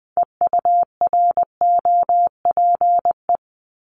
Morse code, heard as a keyed tone, about 20 wpm, 700 Hz. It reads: EUROPE
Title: Multi-letter Phonemes 20wpm